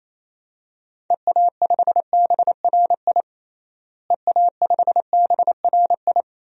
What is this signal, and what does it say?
Morse code, heard as a keyed tone, about 28 wpm, 700 Hz. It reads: EA5BRI EA5BRI
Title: DXCC-100 Call Signs - 6 Characters - Repeated 3x - 28wpm